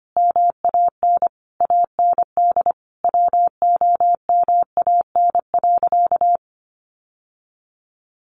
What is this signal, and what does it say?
Morse code, heard as a keyed tone, about 25 wpm, 700 Hz. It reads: MAN AND WOMAN.
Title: Sentences from Top 1100 Words 25wpm